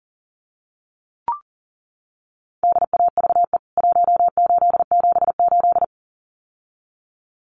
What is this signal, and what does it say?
Morse code, heard as a keyed tone, about 40 wpm, 700 Hz. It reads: DAVE 1878
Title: CWT Exchange 40wpm